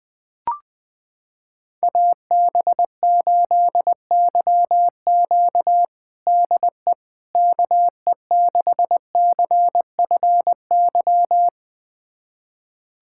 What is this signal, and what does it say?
Morse code, heard as a keyed tone, about 21 wpm, 700 Hz. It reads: AB8YQ DE KE6CFY
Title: USA Callsign DE Callsign 20wpm